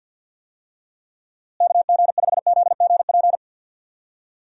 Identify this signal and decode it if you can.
Morse code, heard as a keyed tone, about 52 wpm, 700 Hz. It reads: KG5ZGP
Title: USA Call Signs - 6 Characters - Repeated 3x - 50wpm